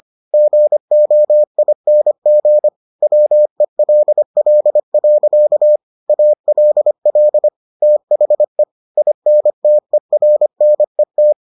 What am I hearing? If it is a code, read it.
Morse code, heard as a keyed tone, about 25 wpm, 600 Hz. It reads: GOING WELL. ALL THE INTERNET